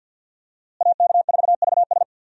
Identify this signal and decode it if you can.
Morse code, heard as a keyed tone, about 50 wpm, 700 Hz. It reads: AK4VS
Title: USA Call Signs - 5 Characters - Repeated 3x - 50wpm